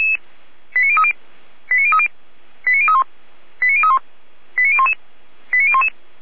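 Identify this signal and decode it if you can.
Five-tone selective call (ZVEI), heard as quick runs of five tones, tones of about 70 ms. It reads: 89030 89030 89031 89031 89011 89011
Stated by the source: Five-tone ZVEI selective call received on SDR.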